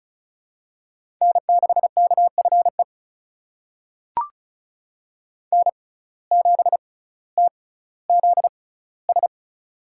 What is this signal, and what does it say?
Morse code, heard as a keyed tone, about 36 wpm, spaced as for 18 wpm, 700 Hz. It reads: N6KFE N7TZS
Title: USA Call Signs - 5 Characters - 6x Character Spacing - 35wpm